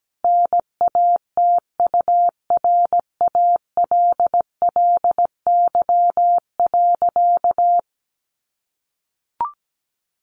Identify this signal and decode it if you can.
Morse code, heard as a keyed tone, about 17 wpm, 700 Hz. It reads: NATURALLY.
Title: Sentences from Top 1500 Words 17wpm